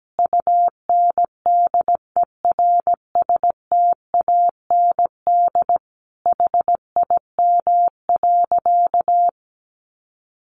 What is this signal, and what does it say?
Morse code, heard as a keyed tone, about 17 wpm, 700 Hz. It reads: UNDERSTAND HIM.
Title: Sentences from Top 600 Words 17wpm